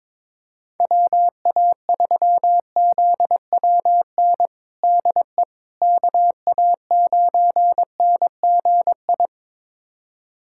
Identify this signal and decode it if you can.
Morse code, heard as a keyed tone, about 22 wpm, 700 Hz. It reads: WA3ZWN DE KA9NGI
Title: USA Callsign DE Callsign 22wpm